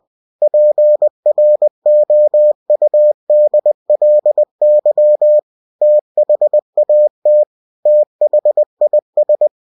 Morse code, heard as a keyed tone, about 20 wpm, 600 Hz. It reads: PROUDLY THAT THIS
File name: cw-books / winnie-the-pooh / winnie-20wpm-page-0085